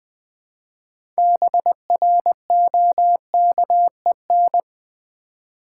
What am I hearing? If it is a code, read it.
Morse code, heard as a keyed tone, about 20 wpm, 700 Hz. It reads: BROKEN
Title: 6 Letter Words - First 3 Letters Given - 20wpm